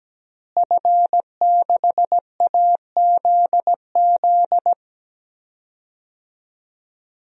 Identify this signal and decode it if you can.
Morse code, heard as a keyed tone, about 17 wpm, 700 Hz. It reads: F6AZZ
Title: DXCC-100 Call Signs - 5 Characters - Repeated 3x - 17wpm